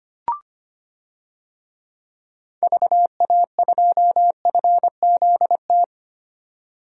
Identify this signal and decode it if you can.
Morse code, heard as a keyed tone, about 25 wpm, 700 Hz. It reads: VA2FZT
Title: DXCC-100 Call Signs - 6 Characters - 25wpm